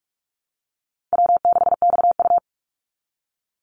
Morse code, heard as a keyed tone, about 44 wpm, 700 Hz. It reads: W6XU
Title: Easy Call Signs Repeated 3x - 45wpm